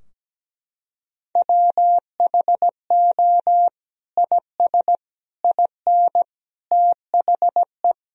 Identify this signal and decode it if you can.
Morse code, heard as a keyed tone, about 17 wpm, 700 Hz. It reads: WHO IS IN THE
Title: Sentences from Top 1000 Words - Review Part 4 - 17wpm